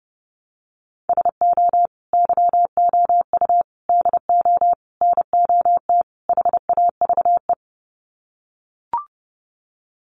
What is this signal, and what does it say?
Morse code, heard as a keyed tone, about 30 wpm, 700 Hz. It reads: SO YOU DO NOT HAVE
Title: Sets of 5 Words Encore 30wpm